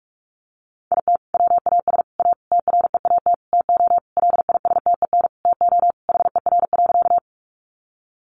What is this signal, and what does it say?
Morse code, heard as a keyed tone, about 46 wpm, 700 Hz. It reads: IT WAS A TREAT TO LISTEN TO HER.
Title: Sentences from Top 1300 Words 45wpm